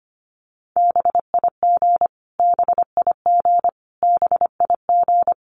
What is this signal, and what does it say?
Morse code, heard as a keyed tone, about 25 wpm, 700 Hz. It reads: BIG BIG BIG
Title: Top 500 Words Repeated 25wpm